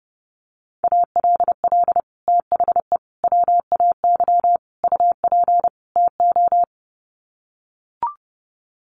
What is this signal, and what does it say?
Morse code, heard as a keyed tone, about 30 wpm, 700 Hz. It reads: ALL THE WAY UP TO
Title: Sets of 5 Words Encore 30wpm